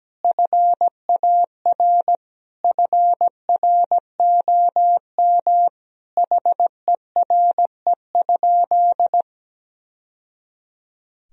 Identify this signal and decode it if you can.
Morse code, heard as a keyed tone, about 17 wpm, 700 Hz. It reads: FAR FROM HERE?
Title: Sentences from Top 1000 Words - Review Part 4 - 17wpm